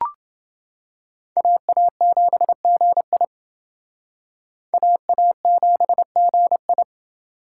Morse code, heard as a keyed tone, about 28 wpm, 700 Hz. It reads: AA7GI AA7GI